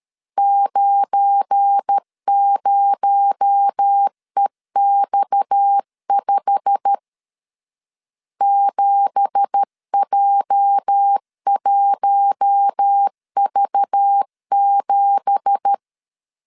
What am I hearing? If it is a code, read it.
Morse code, heard as a keyed tone, about 13 wpm, 800 Hz. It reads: 90EX5 7J1V7
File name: LEZIONI / 9 2867 12wpm 2322